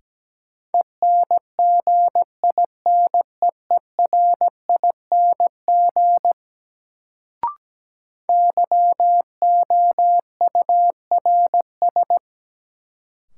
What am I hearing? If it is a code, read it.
Morse code, heard as a keyed tone, about 17 wpm, 700 Hz. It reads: ENGINEERING YOURS
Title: Sentences from Top 1500 Words - New Word Drill Alt - 17wpm